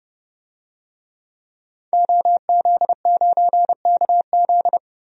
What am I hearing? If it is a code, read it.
Morse code, heard as a keyed tone, about 29 wpm, 700 Hz. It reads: OZ9KZ
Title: DXCC-100 Call Signs - 5 Characters - 30wpm